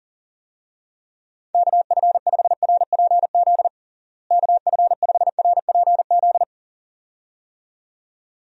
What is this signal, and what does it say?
Morse code, heard as a keyed tone, about 37 wpm, 700 Hz. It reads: KF5RPZ KF5RPZ
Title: USA Call Signs - 6 Characters - Repeated 2x - 40wpm